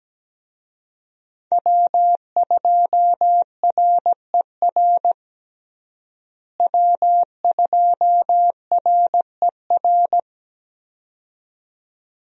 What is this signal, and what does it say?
Morse code, heard as a keyed tone, about 17 wpm, 700 Hz. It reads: W2RER W2RER
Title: USA Call Signs - 5 Characters - Repeated 2x - 17wpm